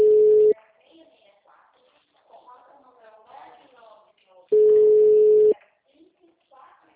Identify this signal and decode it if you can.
Ringback tone (Europe, 425 Hz), with speech.